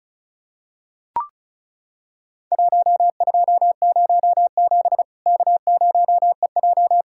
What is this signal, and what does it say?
Morse code, heard as a keyed tone, about 36 wpm, 700 Hz. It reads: 1207 K0EJ
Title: Worked All Europe DX Contest - QTC Exchange - 35wpm